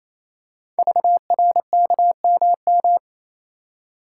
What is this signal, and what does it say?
Morse code, heard as a keyed tone, about 27 wpm, 700 Hz. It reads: VRKMM
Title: Code Groups - 5 Letters - 28wpm